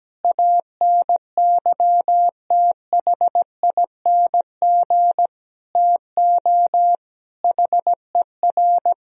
Morse code, heard as a keyed tone, about 17 wpm, 700 Hz. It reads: ANYTHING TO HER
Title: Sentences from Top 1000 Words - Review Part 4 - 17wpm